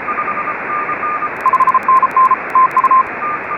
Morse code, heard as a keyed tone, about 33 wpm, 1040 Hz, with radio noise. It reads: 5NN TU
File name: RS44 JA9KRO